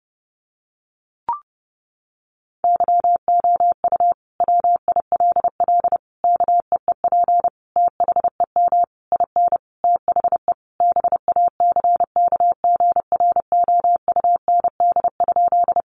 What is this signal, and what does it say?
Morse code, heard as a keyed tone, about 30 wpm, 700 Hz. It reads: YOU WILL KEEP THEM IN THE BACKGROUND?
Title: Sentences from Top 1300 Words 30wpm